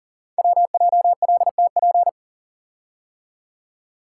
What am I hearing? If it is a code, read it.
Morse code, heard as a keyed tone, about 39 wpm, 700 Hz. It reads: WJLTP
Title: Code Groups - 5 Letters - 40wpm